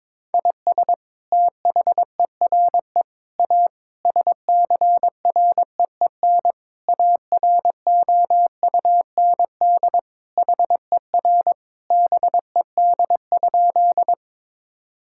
Morse code, heard as a keyed tone, about 21 wpm, 700 Hz. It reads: IS THERE A SCREEN AROUND HER BED?
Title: Sentences from Top 1000 Words - Review Part 6 - 22wpm